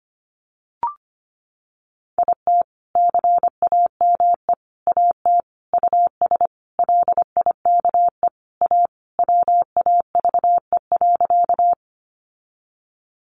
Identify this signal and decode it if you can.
Morse code, heard as a keyed tone, about 25 wpm, 700 Hz. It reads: IT CAME AT US LIKE A WAVE.
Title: Sentences from Top 1400 Words 25wpm